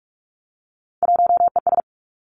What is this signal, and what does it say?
Morse code, heard as a keyed tone, about 46 wpm, 700 Hz. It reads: 1ES
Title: Three Challenging Letter-Number Combinations 45wpm